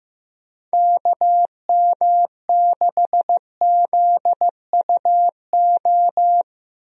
Morse code, heard as a keyed tone, about 15 wpm, 700 Hz. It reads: KM6ZUO
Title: USA Call Signs - 6 Characters - Repeated 2x - 15wpm